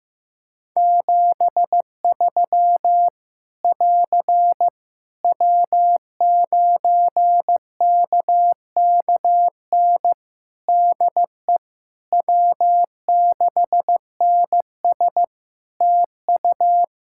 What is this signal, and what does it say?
Morse code, heard as a keyed tone, about 15 wpm, 700 Hz. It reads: 73 <AR> W9KKN DE W6NS TU
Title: Practice QSO - Exchange 4 - Ending - 15wpm